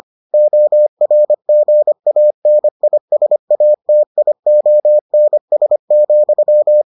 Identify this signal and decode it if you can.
Morse code, heard as a keyed tone, about 25 wpm, 600 Hz. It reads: ORGANISATIONS,